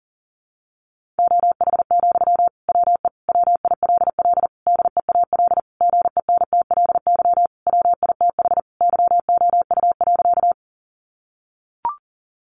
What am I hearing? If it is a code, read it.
Morse code, heard as a keyed tone, about 40 wpm, 700 Hz. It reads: OH, WE WILL DEAL GENTLY WITH YOU.
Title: Sentences from Top 1500 Words 40wpm